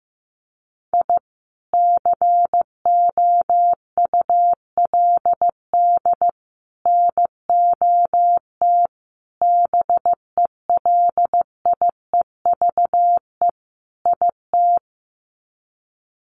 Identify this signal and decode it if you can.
Morse code, heard as a keyed tone, about 15 wpm, 700 Hz. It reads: I COULD NOT BELIEVE IT
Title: Sets of 5 Words Encore 15wpm